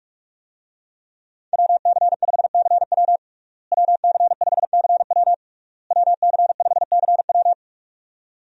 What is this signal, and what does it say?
Morse code, heard as a keyed tone, about 46 wpm, 700 Hz. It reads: WC5CW WC5CW WC5CW
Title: USA Call Signs - 5 Characters - Repeated 3x - 45wpm